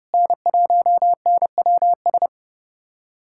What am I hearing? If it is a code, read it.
Morse code, heard as a keyed tone, about 30 wpm, 700 Hz. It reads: N1NWS
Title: USA Callsign DE Callsign - 5 Characters - 30wpm